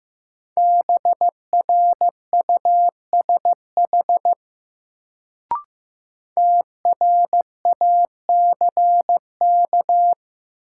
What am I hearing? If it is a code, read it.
Morse code, heard as a keyed tone, about 15 wpm, 700 Hz. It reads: BRUSH TRACK
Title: Sentences from Top 1100 Words - New Word Drill - 15wpm